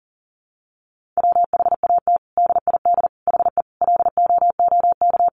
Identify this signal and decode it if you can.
Morse code, heard as a keyed tone, about 40 wpm, 700 Hz. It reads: WHAT DID HE LOOK
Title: Sentences from Top 1300 Words 40wpm